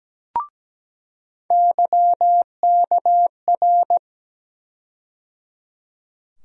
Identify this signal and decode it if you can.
Morse code, heard as a keyed tone, about 17 wpm, 700 Hz. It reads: YKR
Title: USA Call Sign Postfixes - 3 Letters 17wpm